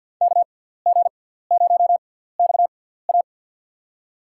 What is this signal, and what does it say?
Morse code, heard as a keyed tone, about 48 wpm, spaced as for 17 wpm, 700 Hz. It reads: KG0XA
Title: USA Call Signs - 6x Character Spacing - 50wpm